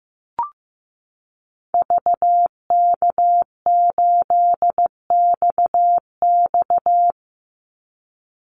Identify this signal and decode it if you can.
Morse code, heard as a keyed tone, about 15 wpm, 700 Hz. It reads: VK8XX
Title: Call Signs Repeated 2x - 15wpm